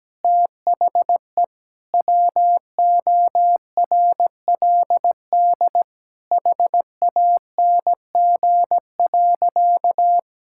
Morse code, heard as a keyed tone, about 17 wpm, 700 Hz. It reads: THE WORLD HANG.
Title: Sentences from Top 1600 Words 17wpm